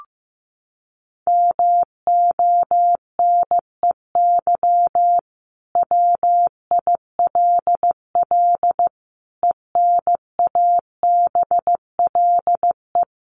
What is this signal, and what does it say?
Morse code, heard as a keyed tone, about 15 wpm, 700 Hz. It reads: MONEY WILL ENABLE
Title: Sentences from Top 1500 Words 15wpm